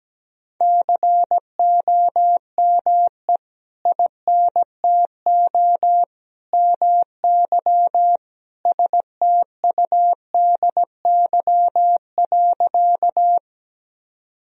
Morse code, heard as a keyed tone, about 17 wpm, 700 Hz. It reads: COME INTO MY STUDY.
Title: Sentences from Top 1000 Words - Review Part 1 - 17wpm